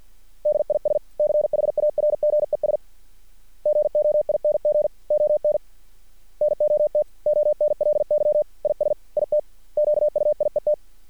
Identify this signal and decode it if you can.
Morse code, heard as a keyed tone, about 50 wpm, 600 Hz. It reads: DISCHARGES GOING ON NOT ONLY IS IT QUIET
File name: archived / CRSnet